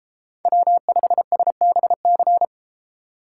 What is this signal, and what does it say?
Morse code, heard as a keyed tone, about 33 wpm, 700 Hz. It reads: W5SBC